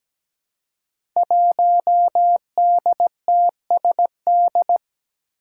Morse code, heard as a keyed tone, about 17 wpm, 700 Hz. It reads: 1DTSD
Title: Five Challenging Letter-Number Combinations 17wpm